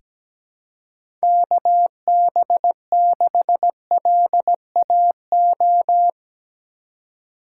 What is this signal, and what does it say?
Morse code, heard as a keyed tone, about 17 wpm, 700 Hz. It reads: KB6LAO
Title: USA Call Signs - 6 Characters - 3x Character Spacing - 17wpm